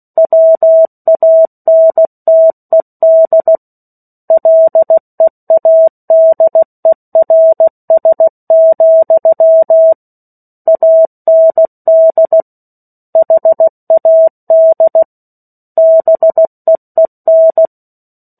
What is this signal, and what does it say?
Morse code, heard as a keyed tone, about 16 wpm, 650 Hz. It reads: WANTED LEADERS, AND HAD BEEN
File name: 03-Chapter-03 16 WPM